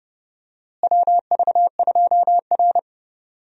Morse code, heard as a keyed tone, about 30 wpm, 700 Hz. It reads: WV2R
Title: USA Call Signs - 4 Characters - Repeated 2x - 30wpm